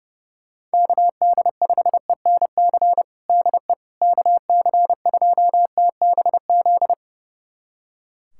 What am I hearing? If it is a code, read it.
Morse code, heard as a keyed tone, about 32 wpm, 700 Hz. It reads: KD5ENC DE KC2TBZ
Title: USA Callsign DE Callsign - 6 Characters - 30wpm